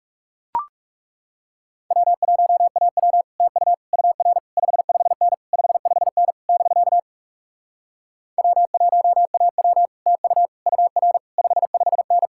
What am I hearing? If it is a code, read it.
Morse code, heard as a keyed tone, about 40 wpm, 700 Hz. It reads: W1AW TU UR 55N 55N <BK> W1AW TU UR 55N